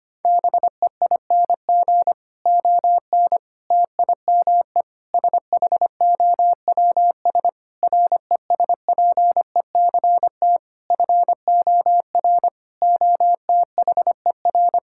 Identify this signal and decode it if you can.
Morse code, heard as a keyed tone, about 25 wpm, 700 Hz. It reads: BEING ON TIME SHOWS RESPECT FOR OTHER
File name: practice / 25wpm / Difficult Sentences with Hints - 25wpm